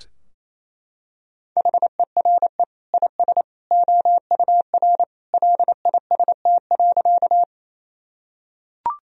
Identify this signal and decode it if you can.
Morse code, heard as a keyed tone, about 28 wpm, 700 Hz. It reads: HERE IS OUR LIST.
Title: Sentences from Top 1000 Words - Review Part 2 - 28wpm